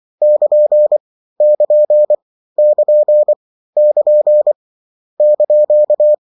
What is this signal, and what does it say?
Morse code, heard as a keyed tone, about 24 wpm, 600 Hz. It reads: <KN> <KN> <KN> <KN> )
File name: morse mp3 / lecon16-vitesse 22